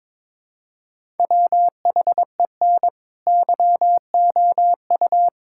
Morse code, heard as a keyed tone, about 22 wpm, 700 Hz. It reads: WHEN YOU
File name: Build A Phrase - 2-Word Phrases - 22wpm